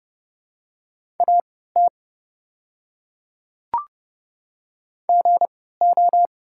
Morse code, heard as a keyed tone, about 30 wpm, spaced as for 17 wpm, 700 Hz. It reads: AT GO
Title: 3x Character Spacing - Top 100 Words - 30wpm